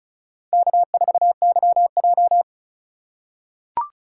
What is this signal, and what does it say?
Morse code, heard as a keyed tone, about 36 wpm, 700 Hz. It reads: K4YJ
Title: USA Call Signs - 4x Character Spacing - 35wpm